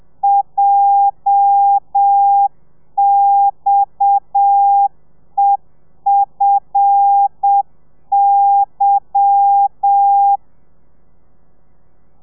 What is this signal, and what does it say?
Morse code, heard as a keyed tone, about 7 wpm, 800 Hz. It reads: JXEFY
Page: Lezione 1: audio a 30 CPM